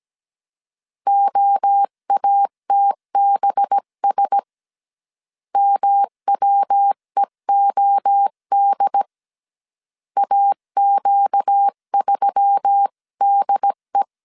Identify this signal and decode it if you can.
Morse code, heard as a keyed tone, about 17 wpm, 800 Hz. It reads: OATBS MWEOD AQ3DE
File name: CW / DET / 1 6 1500C S C 16wpm 2733